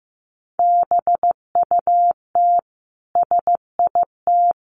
Morse code, heard as a keyed tone, about 15 wpm, 700 Hz. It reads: BUT SIT
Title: Sentences from Top 600 Words 15wpm